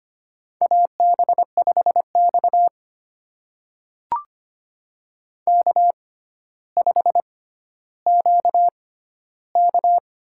Morse code, heard as a keyed tone, about 25 wpm, spaced as for 13 wpm, 700 Hz. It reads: AB5X K5QK
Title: USA Call Signs - 4 Characters - 6x Character Spacing - 25wpm